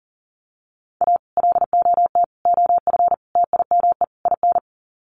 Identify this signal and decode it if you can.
Morse code, heard as a keyed tone, about 40 wpm, 700 Hz. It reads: A LOT OF TIME IN
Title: Sets of 5 Words Encore2 40wpm